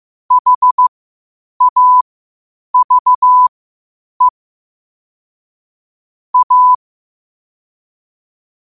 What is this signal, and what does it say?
Morse code, heard as a keyed tone, about 15 wpm, spaced as for 8 wpm, 1000 Hz. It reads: HAVE A
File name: qotdfiles / 8wpm